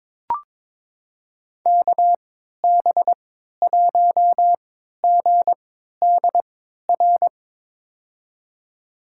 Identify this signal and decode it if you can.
Morse code, heard as a keyed tone, about 21 wpm, spaced as for 12 wpm, 700 Hz. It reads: KB1GDR
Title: USA Call Signs - 6 Characters - 3x Character Spacing - 22wpm